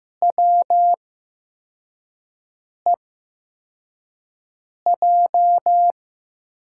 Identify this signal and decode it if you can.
Morse code, heard as a keyed tone, about 15 wpm, spaced as for 4 wpm, 700 Hz. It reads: WEJ
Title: USA Call Signs - 8x Character Spacing - 15wpm